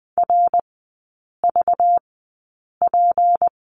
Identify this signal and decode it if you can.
Morse code, heard as a keyed tone, about 20 wpm, spaced as for 8 wpm, 690 Hz. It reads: RVP
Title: 191 Lesson 31e - Letter v - Three Characters 20wpm